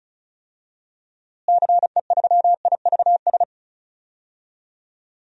Morse code, heard as a keyed tone, about 36 wpm, 700 Hz. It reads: CE3IVS
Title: DXCC-100 Call Signs - 6 Characters - Repeated 2x - 35wpm